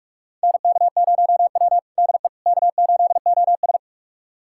Morse code, heard as a keyed tone, about 41 wpm, 700 Hz. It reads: NK0W DE K8OS
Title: USA Callsign DE Callsign - 4 Characters - 45wpm